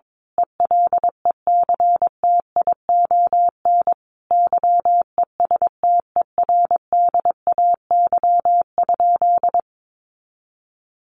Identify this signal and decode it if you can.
Morse code, heard as a keyed tone, about 22 wpm, 700 Hz. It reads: ELECTION YESTERDAY?
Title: A Sound Of Thunder 22wpm